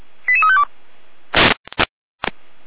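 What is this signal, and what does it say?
Five-tone selective call (ZVEI), heard as quick runs of five tones, tones of about 70 ms. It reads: 88352
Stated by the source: ZVEI five-tone selective call received by SDR.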